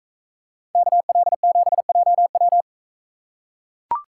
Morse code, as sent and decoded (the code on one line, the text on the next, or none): -.- .-.. --... .--- .--
KL7JW